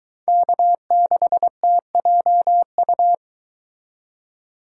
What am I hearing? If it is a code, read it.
Morse code, heard as a keyed tone, about 22 wpm, 700 Hz. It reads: K6TJU